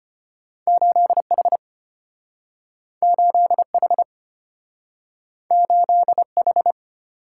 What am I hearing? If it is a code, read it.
Morse code, heard as a keyed tone, about 33 wpm, 700 Hz. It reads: 8H 8H 8H